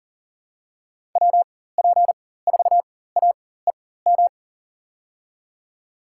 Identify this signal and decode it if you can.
Morse code, heard as a keyed tone, about 40 wpm, spaced as for 19 wpm, 700 Hz. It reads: WP4AEM
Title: USA Call Signs - 6 Characters - 4x Character Spacing - 40wpm